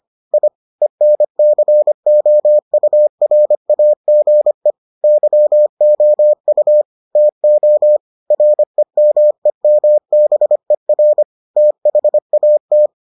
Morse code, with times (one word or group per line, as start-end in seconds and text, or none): 0.3-0.5 I
0.8-4.7 ENCOURAGE
5.0-6.8 YOU
7.2-8.0 TO
8.3-11.2 REMEMBER
11.6-12.9 THAT